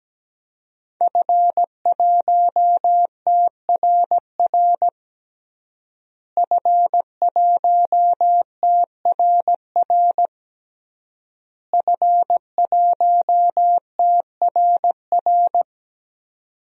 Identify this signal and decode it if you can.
Morse code, heard as a keyed tone, about 17 wpm, 700 Hz. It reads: F1TRR F1TRR F1TRR
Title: DXCC-100 Call Signs - 5 Characters - Repeated 3x - 17wpm